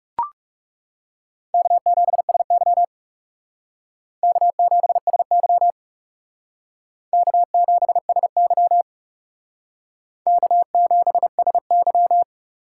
Morse code, heard as a keyed tone, about 38 wpm, 700 Hz. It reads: K7SY K7SY K7SY K7SY